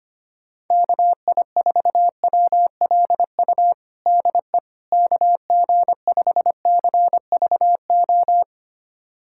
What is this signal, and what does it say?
Morse code, heard as a keyed tone, about 25 wpm, 700 Hz. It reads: KI4WLU DE KG5CVO
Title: USA Callsign DE Callsign - 6 Characters - 25wpm